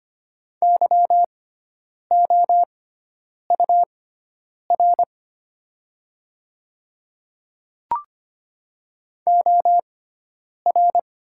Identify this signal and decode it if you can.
Morse code, heard as a keyed tone, about 25 wpm, spaced as for 9 wpm, 690 Hz. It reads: YOUR OR
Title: 6x Character Spacing - Top 100 Words - 25wpm